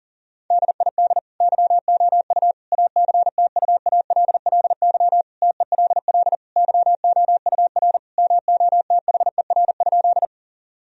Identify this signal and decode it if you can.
Morse code, heard as a keyed tone, about 40 wpm, 700 Hz. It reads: DID YOU ACTUALLY TELL YOUR MOTHER?
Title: Sentences from Top 1000 Words - Review Part 4 - 40wpm